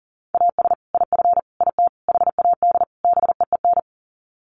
Morse code, heard as a keyed tone, about 40 wpm, 700 Hz. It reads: AS IF IT HAD BEEN
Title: Sets of 5 Words Encore2 40wpm